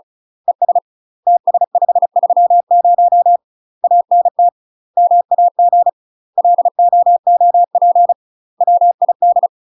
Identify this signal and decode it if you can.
Morse code, heard as a keyed tone, about 34 wpm, 700 Hz. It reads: ES TS530 ANT MAG LOOP WID
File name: UK MASTER QSOs TRAINING / 35WPM / UK MASTER QSOs 0013